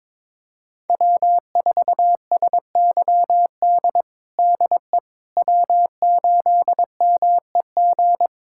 Morse code, heard as a keyed tone, about 22 wpm, 700 Hz. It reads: W4SYD DE W8MEG
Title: USA Callsign DE Callsign - 5 Characters - 22wpm